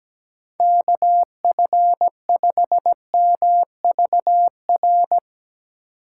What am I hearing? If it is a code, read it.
Morse code, heard as a keyed tone, about 17 wpm, 700 Hz. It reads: KF5MVR
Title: USA Call Signs - 6 Characters - 8x Character Spacing - 17wpm